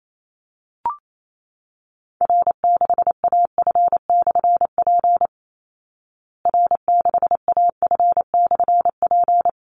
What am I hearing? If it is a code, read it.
Morse code, heard as a keyed tone, about 28 wpm, 700 Hz. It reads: R6AF/P R6AF/P
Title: Call Signs Repeated 2x - 28wpm